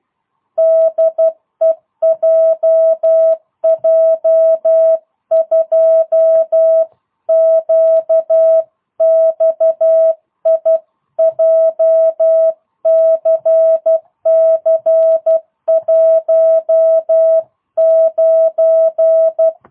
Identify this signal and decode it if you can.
Morse code, heard as a keyed tone, about 12 wpm, 650 Hz. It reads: DE JJ2QXI JCC19
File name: jh8cbh / 2022cw